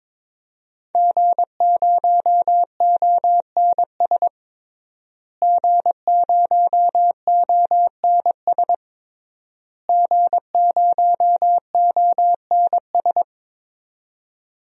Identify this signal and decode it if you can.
Morse code, heard as a keyed tone, about 22 wpm, 700 Hz. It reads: G0ONS G0ONS G0ONS
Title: DXCC-100 Call Signs - 5 Characters - Repeated 3x - 22wpm